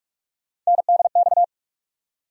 Morse code, heard as a keyed tone, about 46 wpm, 700 Hz. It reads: NDX
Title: USA Call Sign Postfixes - 3 Letters 45wpm